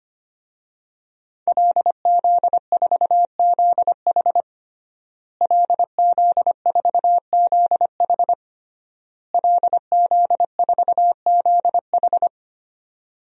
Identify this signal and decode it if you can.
Morse code, heard as a keyed tone, about 25 wpm, 700 Hz. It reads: LZ4ZH LZ4ZH LZ4ZH
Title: DXCC-100 Call Signs - 5 Characters - Repeated 3x - 25wpm